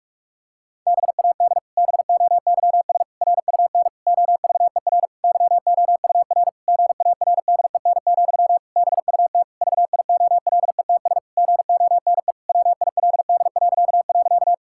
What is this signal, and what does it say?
Morse code, heard as a keyed tone, about 46 wpm, 700 Hz. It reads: BAD BOYS RUN OVER YOUR GARDEN, BUT VIOLETS GONE WILD..
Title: Pangram Challenge v2 - 45wpm